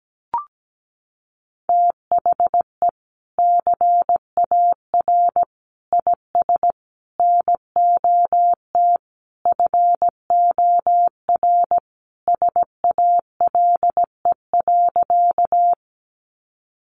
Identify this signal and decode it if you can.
Morse code, heard as a keyed tone, about 17 wpm, 700 Hz. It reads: THE CAR IS NOT FOR SALE.
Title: Sentences from Top 1400 Words 17wpm